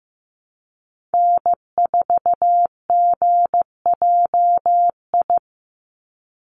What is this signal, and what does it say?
Morse code, heard as a keyed tone, about 15 wpm, 700 Hz. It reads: N4GJI
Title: USA Call Signs 15wpm